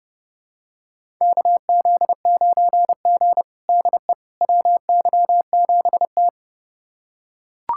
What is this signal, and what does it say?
Morse code, heard as a keyed tone, about 31 wpm, 700 Hz. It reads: KZ9G DE WY7T
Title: USA Callsign DE Callsign - 4 Characters - 30wpm